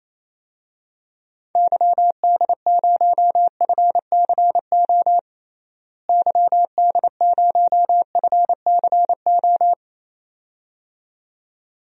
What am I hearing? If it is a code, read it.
Morse code, heard as a keyed tone, about 28 wpm, 700 Hz. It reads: YD0FCO YD0FCO
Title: DXCC-100 Call Signs - 6 Characters - Repeated 2x - 28wpm